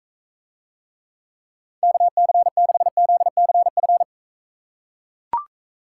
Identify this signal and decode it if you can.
Morse code, heard as a keyed tone, about 41 wpm, 700 Hz. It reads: KC6ZCF